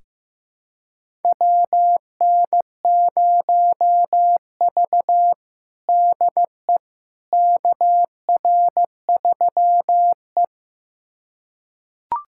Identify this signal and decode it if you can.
Morse code, heard as a keyed tone, about 16 wpm, 700 Hz. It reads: WN0V DE KR3E
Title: USA Callsign DE Callsign - 4 Characters - 15wpm